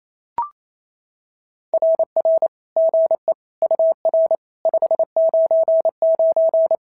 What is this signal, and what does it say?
Morse code, heard as a keyed tone, about 25 wpm, 650 Hz. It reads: RR GE UR 599
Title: SOTA QSO Practice 28wpm